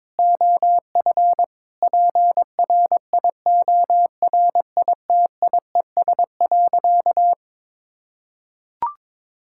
Morse code, heard as a keyed tone, about 22 wpm, 700 Hz. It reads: OF PRIORITIES.
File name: Sentences with 7-Word Phrases - 22wpm